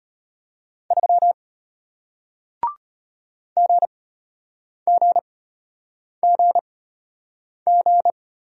Morse code, heard as a keyed tone, about 34 wpm, spaced as for 8 wpm, 700 Hz. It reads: 3 GGGG